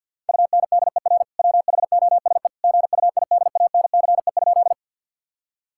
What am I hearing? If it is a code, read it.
Morse code, heard as a keyed tone, about 52 wpm, 700 Hz. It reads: UNDER WHOSE GUIDANCE?
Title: Sentences from Top 1600 Words 50wpm